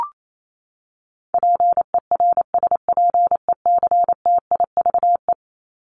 Morse code, heard as a keyed tone, about 28 wpm, 700 Hz. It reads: PERSPECTIVE
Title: Top 1001-1250 Words 28wpm